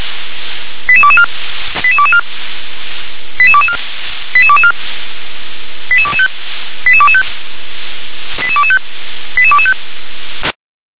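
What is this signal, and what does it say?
Five-tone selective call (ZVEI), heard as quick runs of five tones, tones of about 70 ms. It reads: 88224 88224 88224 88224 88225 88225 88225 88225
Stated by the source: ZVEI five-tone selective call 88222 received on SDR.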